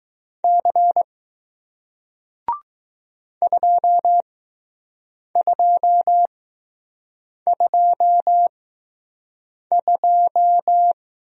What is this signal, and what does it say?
Morse code, heard as a keyed tone, about 20 wpm, spaced as for 6 wpm, 700 Hz. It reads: C 2222